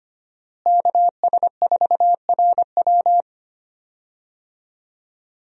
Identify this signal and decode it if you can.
Morse code, heard as a keyed tone, about 25 wpm, 700 Hz. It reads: KS4RW
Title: USA Call Signs - 5 Characters - Repeated 3x - 25wpm